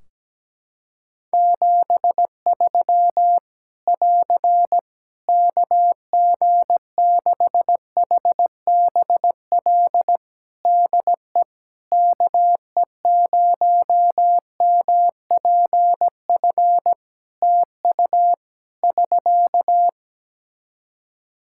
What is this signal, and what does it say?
Morse code, heard as a keyed tone, about 17 wpm, 700 Hz. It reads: 73 <AR> KG6HBL DE KE0MPF TU <SK>
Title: Practice QSO - Exchange 4 - Ending - 17wpm